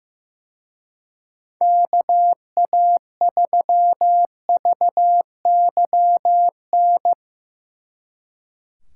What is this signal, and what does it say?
Morse code, heard as a keyed tone, about 17 wpm, 700 Hz. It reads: KA3VYN
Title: Speed Racing - USA Call Signs 15wpm